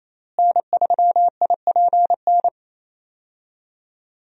Morse code, heard as a keyed tone, about 27 wpm, 700 Hz. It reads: N3IPN
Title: USA Call Signs - 5 Characters - Repeated 3x - 28wpm